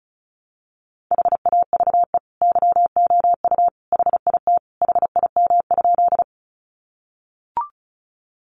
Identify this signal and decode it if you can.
Morse code, heard as a keyed tone, about 34 wpm, 700 Hz. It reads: HAVE YOU HIT HIM?
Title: Sentences from Top 1200 Words 35wpm